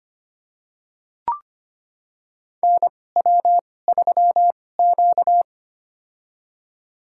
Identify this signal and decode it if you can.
Morse code, heard as a keyed tone, about 25 wpm, spaced as for 18 wpm, 700 Hz. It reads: NW3Q
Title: USA Call Signs - 4 Characters - 2x Character Spacing - 25wpm